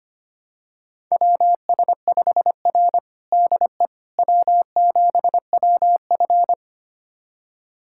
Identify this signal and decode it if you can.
Morse code, heard as a keyed tone, about 25 wpm, 700 Hz. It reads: WS5R DE W7WF
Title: USA Callsign DE Callsign - 4 Characters - 25wpm